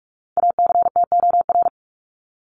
Morse code, heard as a keyed tone, about 44 wpm, 700 Hz. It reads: ACTOR
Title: Top 1251-1500 Words 45wpm